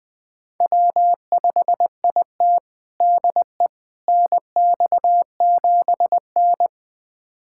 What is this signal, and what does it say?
Morse code, heard as a keyed tone, about 20 wpm, 700 Hz. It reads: W5IT DE NX7N
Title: USA Callsign DE Callsign - 4 Characters - 20wpm